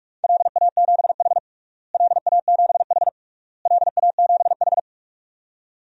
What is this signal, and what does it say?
Morse code, heard as a keyed tone, about 46 wpm, 700 Hz. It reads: LA7H LA7H LA7H
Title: DXCC-100 Call Signs - 4 Characters - Repeated 3x - 45wpm